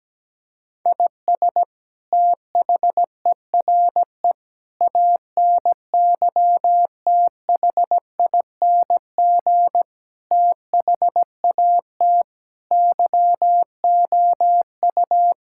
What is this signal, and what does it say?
Morse code, heard as a keyed tone, about 17 wpm, 700 Hz. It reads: IS THERE ANYTHING THAT YOU
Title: Sentences from Top 1000 Words - Review Part 6 - 17wpm